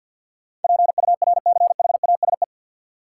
Morse code, heard as a keyed tone, about 53 wpm, 700 Hz. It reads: PURCHASE
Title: Sentences from Top 800 Words - New Word Drill - 50wpm